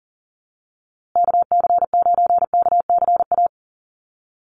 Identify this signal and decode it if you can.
Morse code, heard as a keyed tone, about 40 wpm, 700 Hz. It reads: KC9KCA
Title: USA Call Signs Repeated 3x 40wpm